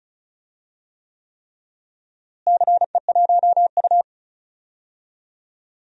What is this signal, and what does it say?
Morse code, heard as a keyed tone, about 34 wpm, 700 Hz. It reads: CE1U
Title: DXCC-100 Call Signs - 4 Characters - Repeated 3x - 35wpm